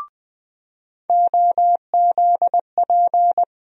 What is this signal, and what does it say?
Morse code, heard as a keyed tone, about 20 wpm, 700 Hz. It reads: OZP
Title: USA Call Sign Postfixes - 3 Letters 20wpm